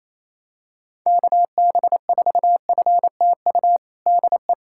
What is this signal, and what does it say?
Morse code, heard as a keyed tone, about 28 wpm, 700 Hz. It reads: KB4FTU DE
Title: USA Callsign DE Callsign - 6 Characters - 28wpm